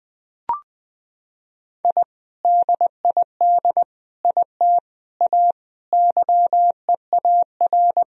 Morse code, heard as a keyed tone, about 20 wpm, 700 Hz. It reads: I DID IT A YEAR
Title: Sentences from Top 1000 Words - Review Part 3 - 20wpm